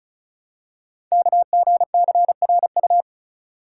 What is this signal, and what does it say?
Morse code, heard as a keyed tone, about 36 wpm, 690 Hz. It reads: KGCRU
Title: Code Groups - 5 Letters - 35wpm